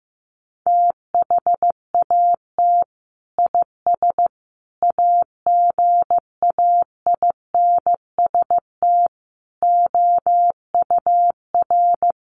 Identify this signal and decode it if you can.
Morse code, heard as a keyed tone, about 15 wpm, 700 Hz. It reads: THAT IS AGAINST OUR
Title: Sentences from Top 1500 Words 15wpm